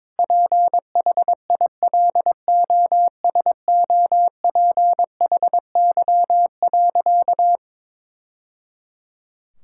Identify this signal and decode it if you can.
Morse code, heard as a keyed tone, about 22 wpm, 700 Hz. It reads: PHILOSOPHY.
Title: Sentences from Top 1000 Words - Review Part 6 - 22wpm